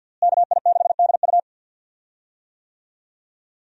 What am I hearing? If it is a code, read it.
Morse code, heard as a keyed tone, about 46 wpm, 700 Hz. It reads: KI6DU
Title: USA Call Signs - 5 Characters - Repeated 2x - 50wpm